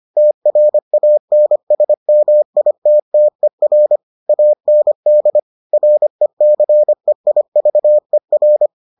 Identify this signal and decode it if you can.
Morse code, heard as a keyed tone, about 25 wpm, 600 Hz. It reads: TRANSMITTER AND RECEIVER